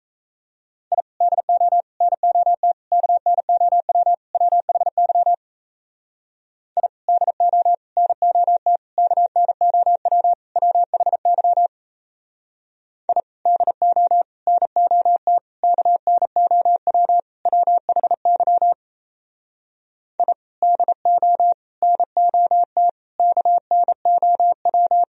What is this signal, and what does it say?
Morse code, heard as a keyed tone, about 36 wpm, 700 Hz. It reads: I DO NOT KNOW WHY I DO NOT KNOW WHY I DO NOT KNOW WHY I DO NOT KNOW